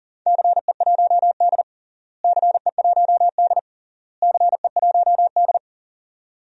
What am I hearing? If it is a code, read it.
Morse code, heard as a keyed tone, about 40 wpm, 700 Hz. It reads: CE1D CE1D CE1D
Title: DXCC-100 Call Signs - 4 Characters - Repeated 3x - 40wpm